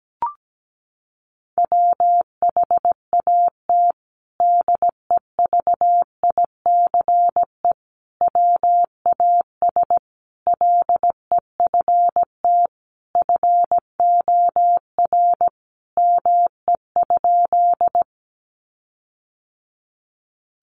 Morse code, heard as a keyed tone, about 17 wpm, 700 Hz. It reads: WHAT DEVICE WAS LEFT FOR ME?
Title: Sentences from Top 1500 Words 17wpm